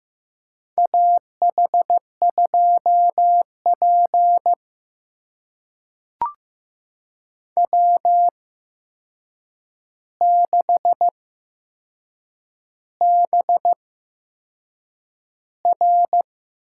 Morse code, heard as a keyed tone, about 15 wpm, spaced as for 6 wpm, 700 Hz. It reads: AH2P W6BR
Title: USA Call Signs - 4 Characters - 8x Character Spacing - 15wpm